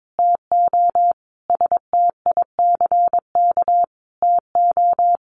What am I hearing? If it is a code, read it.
Morse code, heard as a keyed tone, about 22 wpm, 700 Hz. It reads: TO STICK TO
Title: Sentences from Top 1200 Words 22wpm